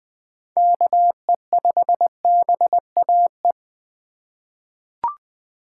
Morse code, heard as a keyed tone, about 21 wpm, 700 Hz. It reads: KE5BAE
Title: USA Call Signs - 4x Character Spacing - 20wpm